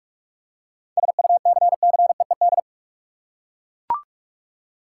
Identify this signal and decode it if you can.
Morse code, heard as a keyed tone, about 42 wpm, 700 Hz. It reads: SUCCEED
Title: Seven Letter Words - 45wpm